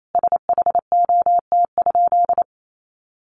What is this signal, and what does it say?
Morse code, heard as a keyed tone, about 28 wpm, 700 Hz. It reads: SHOT?
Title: Sentences from Top 1100 Words 28wpm